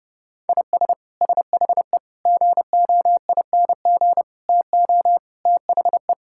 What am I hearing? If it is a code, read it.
Morse code, heard as a keyed tone, about 30 wpm, 700 Hz. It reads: IS SHE GOING TO THE
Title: Sentences from Top 1000 Words - Review Part 6 - 30wpm